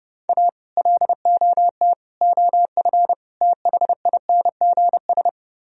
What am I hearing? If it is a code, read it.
Morse code, heard as a keyed tone, about 38 wpm, 700 Hz. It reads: A LOT OF THINGS